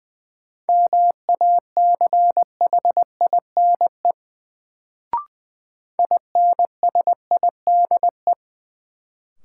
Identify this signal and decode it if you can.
Morse code, heard as a keyed tone, about 20 wpm, 700 Hz. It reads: MACHINE INSIDE
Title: Sentences from Top 1000 Words - Review Part 2 - New Word Drill - 20wpm